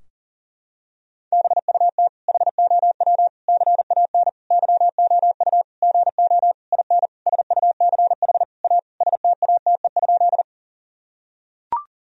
Morse code, heard as a keyed tone, about 40 wpm, 700 Hz. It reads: BUT HOW CAN YOU GO IN SUCH A STATE?
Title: Sentences from Top 1000 Words - Review Part 2 - 40wpm